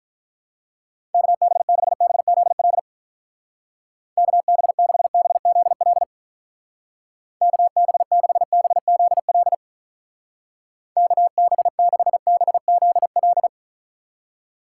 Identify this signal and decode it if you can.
Morse code, heard as a keyed tone, about 43 wpm, 700 Hz. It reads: KB6BZL KB6BZL KB6BZL KB6BZL
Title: Speed Racing - USA Call Signs 35wpm